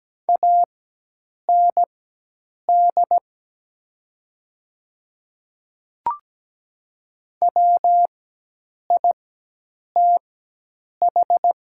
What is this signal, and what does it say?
Morse code, heard as a keyed tone, about 17 wpm, spaced as for 8 wpm, 700 Hz. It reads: AND WITH
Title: 4x Character Spacing - Top 200 Words - 17wpm